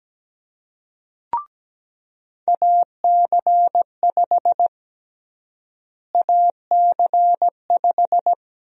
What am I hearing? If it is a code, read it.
Morse code, heard as a keyed tone, about 17 wpm, 700 Hz. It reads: AC5 AC5
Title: USA Call Sign Prefixes - Repeated 2x - 17wpm